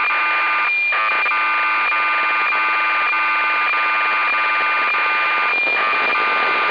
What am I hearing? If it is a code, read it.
Cassette data (Kansas City type).